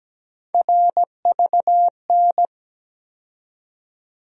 Morse code, heard as a keyed tone, about 17 wpm, 700 Hz. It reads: RVN
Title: USA Call Sign Postfixes - 3 Letters 17wpm